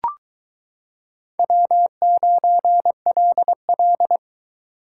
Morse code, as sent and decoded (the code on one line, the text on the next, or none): .-- ----. .-.. .-..
W9LL